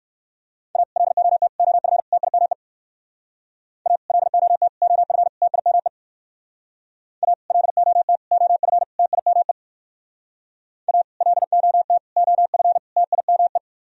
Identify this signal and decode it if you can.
Morse code, heard as a keyed tone, about 57 wpm, 700 Hz. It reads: A LOT OF TIME A LOT OF TIME A LOT OF TIME A LOT OF TIME